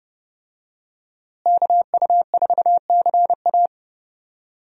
Morse code, heard as a keyed tone, about 30 wpm, 700 Hz. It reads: KU4CA
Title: USA Call Signs - 5 Characters - Repeated 3x - 30wpm